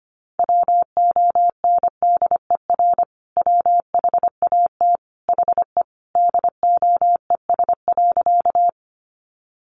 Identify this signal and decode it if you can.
Morse code, heard as a keyed tone, about 25 wpm, 700 Hz. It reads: WONDER WHAT HE DOES.
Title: Sentences from Top 500 Words 25wpm